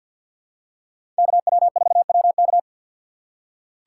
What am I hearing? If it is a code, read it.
Morse code, heard as a keyed tone, about 47 wpm, 700 Hz. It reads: KW4WK
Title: USA Call Signs - 5 Characters - 3x Character Spacing - 50wpm